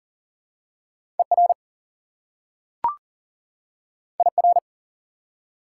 Morse code, heard as a keyed tone, about 40 wpm, 700 Hz. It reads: ER IR
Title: Prefixes and Suffixes 40wpm